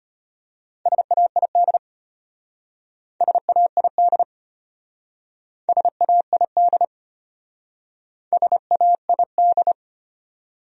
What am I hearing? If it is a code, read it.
Morse code, heard as a keyed tone, about 31 wpm, 700 Hz. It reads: SAID SAID SAID SAID
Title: Speed-Racing Top 100 Words 25wpm